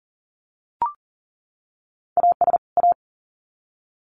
Morse code, heard as a keyed tone, about 40 wpm, 700 Hz. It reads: AS A
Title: Sets of 2 Words 40wpm